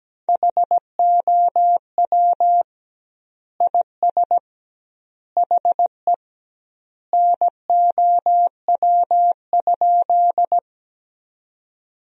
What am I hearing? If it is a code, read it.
Morse code, heard as a keyed tone, about 17 wpm, 700 Hz. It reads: HOW IS HE NOW?
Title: 2x Word-Spacing - Sentences from Top 100 Words - 17wpm